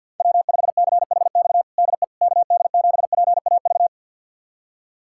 Morse code, heard as a keyed tone, about 46 wpm, 700 Hz. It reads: W5CHX DE KD7PAV
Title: USA Callsign DE Callsign 50wpm